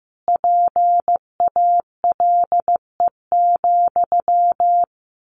Morse code, heard as a keyed tone, about 15 wpm, 700 Hz. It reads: PALE,
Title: A Sound Of Thunder 15wpm